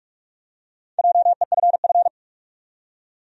Morse code, heard as a keyed tone, about 47 wpm, 700 Hz. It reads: JEFF
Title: USA Common Names - Max 5 Letters - 45wpm